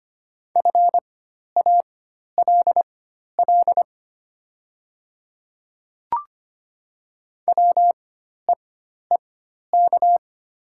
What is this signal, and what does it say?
Morse code, heard as a keyed tone, about 25 wpm, spaced as for 12 wpm, 700 Hz. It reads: FALL WEEK
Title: 4x Character Spacing - Top 500 Words - 25wpm